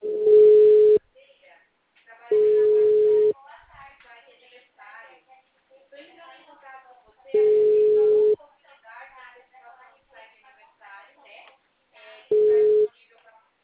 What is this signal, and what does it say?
Ringback tone (Europe, 425 Hz), with speech.